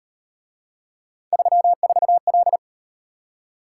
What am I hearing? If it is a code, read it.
Morse code, heard as a keyed tone, about 38 wpm, 700 Hz. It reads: 34L